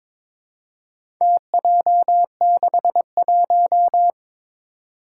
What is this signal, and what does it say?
Morse code, heard as a keyed tone, about 22 wpm, 700 Hz. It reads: TJ61
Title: Four Challenging Letter-Number Combinations 22wpm